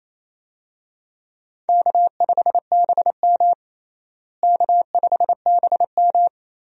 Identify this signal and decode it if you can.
Morse code, heard as a keyed tone, about 27 wpm, 700 Hz. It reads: K5BM K5BM
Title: USA Call Signs - 4 Characters - Repeated 3x - 28wpm